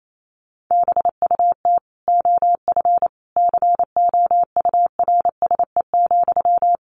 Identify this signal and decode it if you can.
Morse code, heard as a keyed tone, about 28 wpm, 700 Hz. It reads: BUT OF COURSE,
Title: Sentences from Top 1500 Words 28wpm